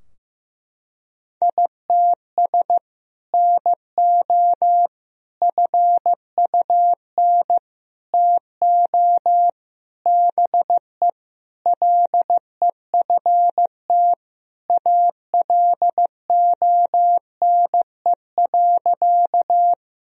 Morse code, heard as a keyed tone, about 15 wpm, 700 Hz. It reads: ITS NO FUN TO BE LEFT ALONE.
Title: Sentences from Top 1000 Words - Review Part 4 - 15wpm